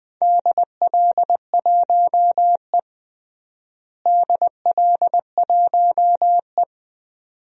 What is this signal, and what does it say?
Morse code, heard as a keyed tone, about 20 wpm, 700 Hz. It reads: DL1E DL1E
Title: DXCC-100 Call Signs - 4 Characters - Repeated 2x - 20wpm